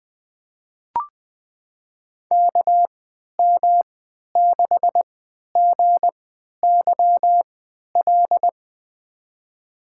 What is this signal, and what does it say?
Morse code, heard as a keyed tone, about 21 wpm, spaced as for 11 wpm, 700 Hz. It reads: KM6GYL
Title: USA Call Signs - 6 Characters - 3x Character Spacing - 20wpm